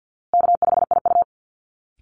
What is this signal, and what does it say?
Morse code, heard as a keyed tone, about 50 wpm, 700 Hz. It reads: K5IU
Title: Easy Call Signs Repeated 2x - 50wpm